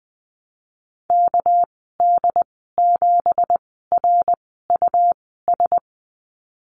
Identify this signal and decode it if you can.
Morse code, heard as a keyed tone, about 20 wpm, spaced as for 15 wpm, 700 Hz. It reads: KD7RUS
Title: 207 Lesson 33g - Letter j - Call Signs 20wpm